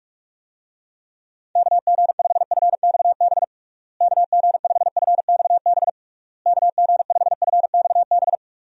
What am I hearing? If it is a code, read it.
Morse code, heard as a keyed tone, about 46 wpm, 700 Hz. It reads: KG5FXB KG5FXB KG5FXB
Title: USA Call Signs - 6 Characters - Repeated 3x - 45wpm